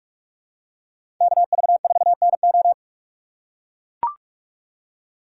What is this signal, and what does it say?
Morse code, heard as a keyed tone, about 44 wpm, 700 Hz. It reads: KV4NO
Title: USA Call Signs - 5 Characters - 3x Character Spacing - 45wpm